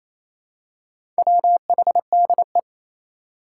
Morse code, heard as a keyed tone, about 28 wpm, 700 Hz. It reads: WHDE
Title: Four Challenging Letter-Number Combinations 28wpm